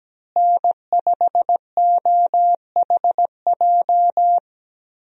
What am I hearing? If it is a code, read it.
Morse code, heard as a keyed tone, about 17 wpm, 700 Hz. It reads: N5OHJ
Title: USA Call Signs - 5 Characters - Repeated 3x - 17wpm